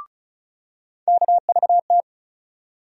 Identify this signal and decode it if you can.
Morse code, heard as a keyed tone, about 36 wpm, 700 Hz. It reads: KVT
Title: USA Call Sign Postfixes - 3 Letters 35wpm